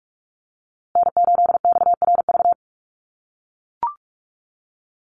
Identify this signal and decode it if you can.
Morse code, heard as a keyed tone, about 44 wpm, 700 Hz. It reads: N8XRV
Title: USA Call Signs Repeated 3x 45wpm